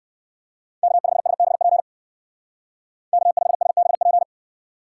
Morse code, heard as a keyed tone, about 59 wpm, 700 Hz. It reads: K5SBP K5SBP